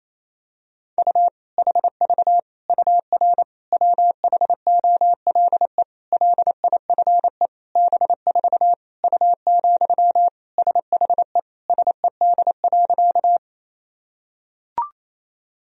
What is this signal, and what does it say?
Morse code, heard as a keyed tone, about 27 wpm, 700 Hz. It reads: U HV UR WHOLE LIFE B4 U, SHE SED.
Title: Sentences from Top 400 Words and Abbreviations 28wpm